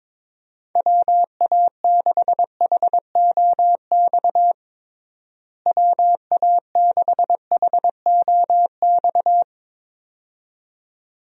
Morse code, heard as a keyed tone, about 22 wpm, 700 Hz. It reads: WA6HOX WA6HOX
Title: USA Call Signs - 6 Characters - Repeated 2x - 22wpm